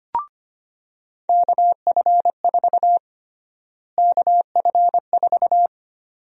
Morse code, heard as a keyed tone, about 25 wpm, 700 Hz. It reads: KF4 KF4
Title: USA Call Sign Prefixes - Repeated 2x - 25wpm